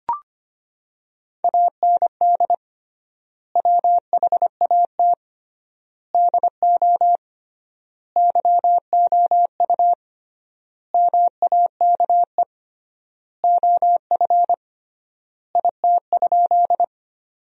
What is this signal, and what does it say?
Morse code, heard as a keyed tone, about 25 wpm, 700 Hz. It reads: AND WHAT DO YOU MAKE OF IT?
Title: 3x Word-Spacing - Sentences from Top 100 Words - 25wpm